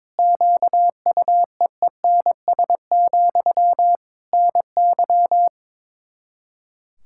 Morse code, heard as a keyed tone, about 22 wpm, 700 Hz. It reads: QUEENS, NY
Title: Practice QSO - Exchange 1 - Beginning - 22wpm